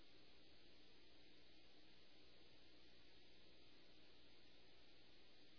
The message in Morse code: none

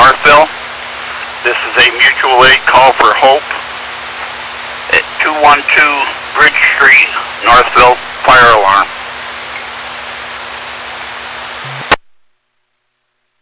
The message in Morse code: none

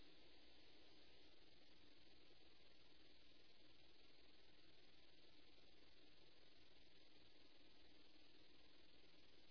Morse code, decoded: none

